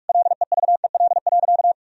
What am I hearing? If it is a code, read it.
Morse code, heard as a keyed tone, about 46 wpm, 700 Hz. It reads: LEVEL.
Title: Sentences from Top 1000 Words - Review Part 4 - 45wpm